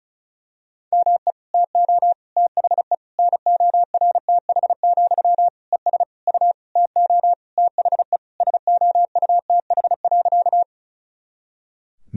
Morse code, heard as a keyed tone, about 36 wpm, 700 Hz. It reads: ME TO THE NORTH, ES U TO THE SOUTH.
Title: Sentences from Top 400 Words and Abbreviations 35wpm